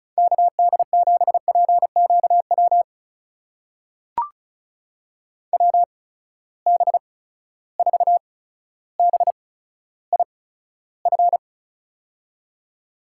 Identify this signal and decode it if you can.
Morse code, heard as a keyed tone, about 34 wpm, spaced as for 15 wpm, 700 Hz. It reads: KD7PQW WB4BIF
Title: USA Call Signs - 6 Characters - 8x Character Spacing - 35wpm